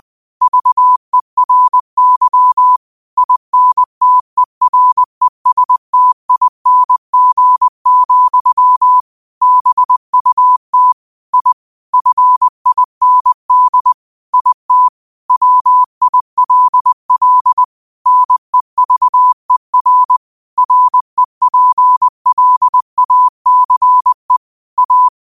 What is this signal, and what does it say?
Morse code, heard as a keyed tone, about 20 wpm, 1000 Hz. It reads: VERY INTERESTING, BUT I FIND IT WILL NEVER REPLACE A
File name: 20wpm